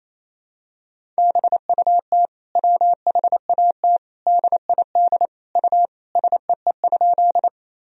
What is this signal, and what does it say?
Morse code, heard as a keyed tone, about 28 wpm, 700 Hz. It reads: BUT WHAT DID U SEE?
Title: Sentences from Top 100 Words and Abbreviations 28wpm